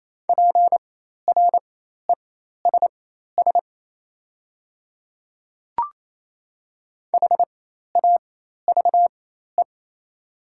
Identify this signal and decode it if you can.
Morse code, heard as a keyed tone, about 28 wpm, spaced as for 13 wpm, 700 Hz. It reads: PRESS HAVE
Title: 4x Character Spacing - Top 500 Words - 28wpm